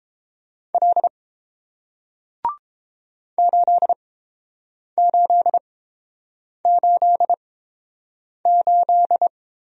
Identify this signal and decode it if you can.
Morse code, heard as a keyed tone, about 30 wpm, spaced as for 7 wpm, 700 Hz. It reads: L 8888